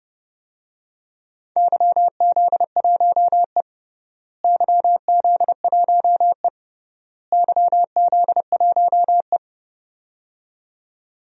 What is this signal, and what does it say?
Morse code, heard as a keyed tone, about 32 wpm, 700 Hz. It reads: YZ1E YZ1E YZ1E
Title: DXCC-100 Call Signs - 4 Characters - Repeated 3x - 30wpm